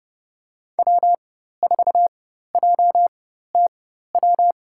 Morse code, heard as a keyed tone, about 29 wpm, spaced as for 14 wpm, 700 Hz. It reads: W4JTW
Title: USA Call Signs - 5 Characters - 4x Character Spacing - 30wpm